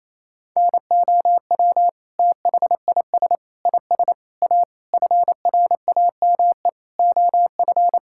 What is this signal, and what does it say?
Morse code, heard as a keyed tone, about 27 wpm, 700 Hz. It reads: NOW THIS IS A FRAME OF
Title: Sentences from Top 1000 Words - Review Part 6 - 28wpm